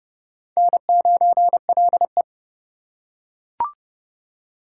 Morse code, heard as a keyed tone, about 25 wpm, 700 Hz. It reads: N9LE